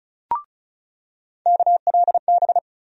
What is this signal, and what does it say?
Morse code, heard as a keyed tone, about 34 wpm, 700 Hz. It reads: KLB
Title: USA Call Sign Postfixes - 3 Letters 35wpm